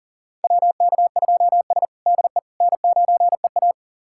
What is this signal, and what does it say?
Morse code, heard as a keyed tone, about 40 wpm, 700 Hz. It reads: WK2S DE N9EA
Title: USA Callsign DE Callsign - 4 Characters - 40wpm